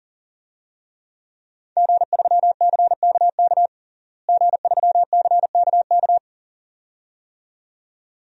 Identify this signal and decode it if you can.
Morse code, heard as a keyed tone, about 37 wpm, 700 Hz. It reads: G3CKK G3CKK
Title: DXCC-100 Call Signs - 5 Characters - Repeated 2x - 40wpm